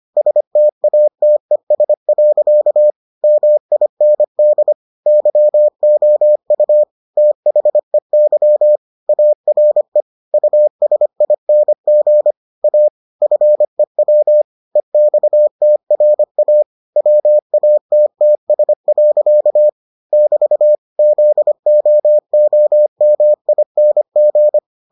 Morse code, heard as a keyed tone, about 25 wpm, 600 Hz. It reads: STATES. MIND YOU THEY ARE USING A FEW EXTRA WATTS. <BT> ZOOMING